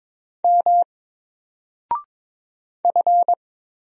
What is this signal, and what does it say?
Morse code, heard as a keyed tone, about 22 wpm, 700 Hz. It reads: M F
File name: Single-Letter-Number-22wpm